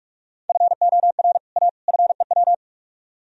Morse code, heard as a keyed tone, about 44 wpm, 700 Hz. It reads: FOR A FEW